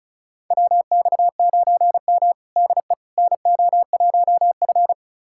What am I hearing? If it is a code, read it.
Morse code, heard as a keyed tone, about 34 wpm, 700 Hz. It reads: WX9M DE NO1F
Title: USA Callsign DE Callsign - 4 Characters - 35wpm